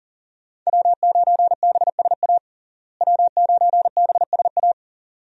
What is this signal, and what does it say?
Morse code, heard as a keyed tone, about 40 wpm, 700 Hz. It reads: W9BSA W9BSA
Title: USA Call Signs - 5 Characters - Repeated 2x - 40wpm